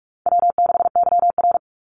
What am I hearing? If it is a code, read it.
Morse code, heard as a keyed tone, about 44 wpm, 700 Hz. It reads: W6YR
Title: Easy Call Signs - 45wpm